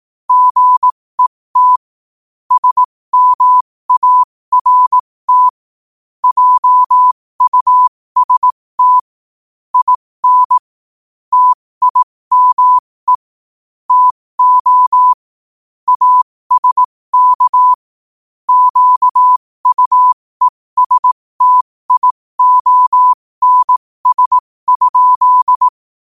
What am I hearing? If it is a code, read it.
Morse code, heard as a keyed tone, about 18 wpm, spaced as for 15 wpm, 1000 Hz. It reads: GET SMART JUST IN TIME TO ASK QUESTIONS?
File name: qotdfiles / 15wpm